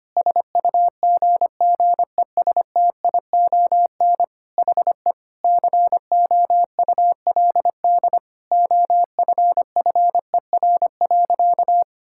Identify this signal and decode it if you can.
Morse code, heard as a keyed tone, about 25 wpm, 700 Hz. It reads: SUGGESTION HE COULD OFFER.
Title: Sentences from Top 1600 Words 25wpm